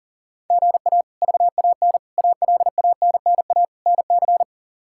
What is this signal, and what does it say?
Morse code, heard as a keyed tone, about 40 wpm, 700 Hz. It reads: GA VAN ALANNA NC
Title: K1USN SST QSO Practice - 40wpm